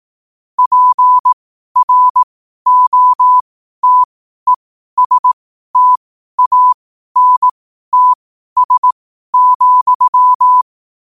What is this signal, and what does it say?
Morse code, heard as a keyed tone, about 18 wpm, spaced as for 12 wpm, 1000 Hz. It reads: PROTESTANTS,
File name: qotdfiles / 12wpm